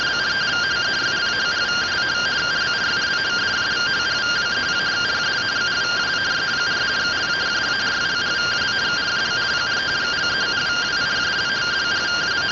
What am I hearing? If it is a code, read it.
Two-tone teleprinter signal, about 170 Hz shift.